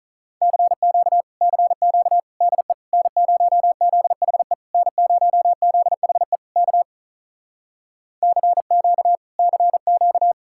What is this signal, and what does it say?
Morse code, heard as a keyed tone, about 46 wpm, 700 Hz. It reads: CQ CQ DE N0ZHE N0ZHE K CQ CQ